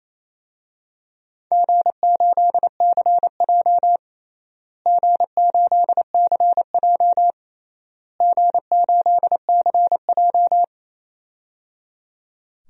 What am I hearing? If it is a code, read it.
Morse code, heard as a keyed tone, about 27 wpm, 700 Hz. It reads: G8CJ G8CJ G8CJ
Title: DXCC-100 Call Signs - 4 Characters - Repeated 3x - 28wpm